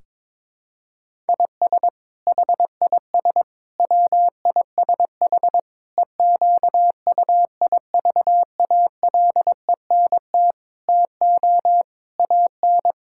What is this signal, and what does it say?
Morse code, heard as a keyed tone, about 22 wpm, 700 Hz. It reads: IS HIS WISH EQUIVALENT TO AN
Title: Sentences from Top 1600 Words 22wpm